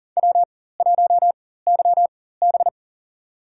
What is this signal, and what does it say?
Morse code, heard as a keyed tone, about 40 wpm, spaced as for 19 wpm, 700 Hz. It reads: W1YB
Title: USA Call Signs - 4 Characters - 4x Character Spacing - 40wpm